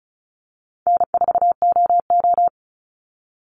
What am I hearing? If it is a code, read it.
Morse code, heard as a keyed tone, about 34 wpm, 700 Hz. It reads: N4OO
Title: Call Signs Repeated 3x - 35wpm